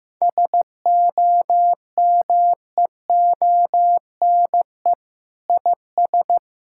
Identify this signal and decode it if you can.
Morse code, heard as a keyed tone, about 15 wpm, 700 Hz. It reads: SOMEONE IS
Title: Sentences from Top 1000 Words - Review Part 6 - 15wpm